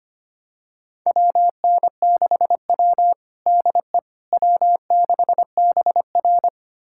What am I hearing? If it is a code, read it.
Morse code, heard as a keyed tone, about 25 wpm, 700 Hz. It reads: WN6W DE W6BR
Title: USA Callsign DE Callsign - 4 Characters - 25wpm